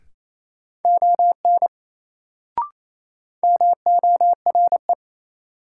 Morse code, heard as a keyed tone, about 28 wpm, 700 Hz. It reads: ON MORE
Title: Sentences from Top 100 Words - New Word Drill - 28wpm